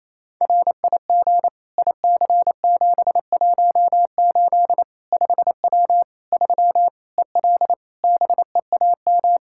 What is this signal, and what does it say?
Morse code, heard as a keyed tone, about 28 wpm, 700 Hz. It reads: RIG IC718 5W 3 EL BEAM
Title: K1USN SST QSO Practice 28wpm